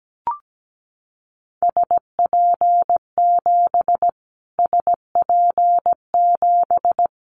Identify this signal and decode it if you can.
Morse code, heard as a keyed tone, about 17 wpm, 700 Hz. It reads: SP7 SP7
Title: Callsign Prefixes for Top 25 DXCC - Repeated 3x - 17wpm